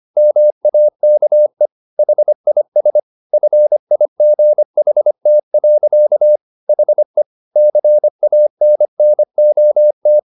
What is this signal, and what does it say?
Morse code, heard as a keyed tone, about 25 wpm, 600 Hz. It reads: MAKE HIS FIGHT. HE CANNOT
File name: the old man and the sea part 1 25wpm